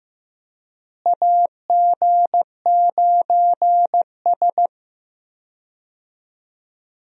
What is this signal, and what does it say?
Morse code, heard as a keyed tone, about 15 wpm, 700 Hz. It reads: AG9S
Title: USA Call Signs - 4 Characters - Repeated 2x - 15wpm